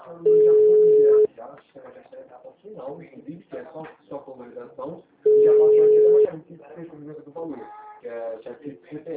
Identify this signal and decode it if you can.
Ringback tone (Europe, 425 Hz), with speech.
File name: out- -2005- - - .25150